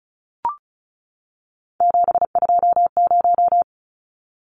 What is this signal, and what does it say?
Morse code, heard as a keyed tone, about 34 wpm, 700 Hz. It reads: 720
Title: Numbers - 3-Digit 35wpm